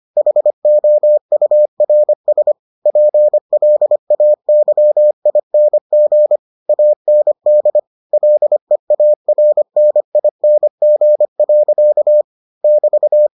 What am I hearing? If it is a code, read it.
Morse code, heard as a keyed tone, about 25 wpm, 600 Hz. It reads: HOURS PLAYING AND LEARNING. <BT>